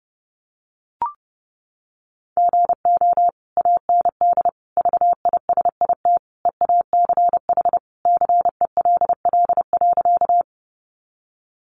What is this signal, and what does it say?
Morse code, heard as a keyed tone, about 28 wpm, 700 Hz. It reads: GO AND VISIT EACH CELL.
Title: Sentences from Top 1100 Words 30wpm